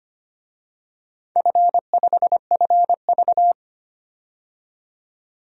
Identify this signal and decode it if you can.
Morse code, heard as a keyed tone, about 25 wpm, 700 Hz. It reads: F5FV
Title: DXCC-100 Call Signs - 4 Characters - 25wpm